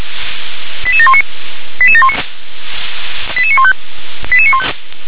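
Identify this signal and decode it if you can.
Five-tone selective call (ZVEI), heard as quick runs of five tones, tones of about 70 ms. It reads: 88719 88719 88915 88915
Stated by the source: ZVEI five-tone selective call received with an SDR.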